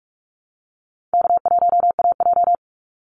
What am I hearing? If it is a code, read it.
Morse code, heard as a keyed tone, about 44 wpm, 700 Hz. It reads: K1AJ
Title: Easy Call Signs Repeated 2x - 45wpm